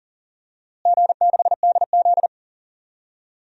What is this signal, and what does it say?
Morse code, heard as a keyed tone, about 40 wpm, 700 Hz. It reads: G6DZ
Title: DXCC-100 Call Signs - 4 Characters - Repeated 3x - 40wpm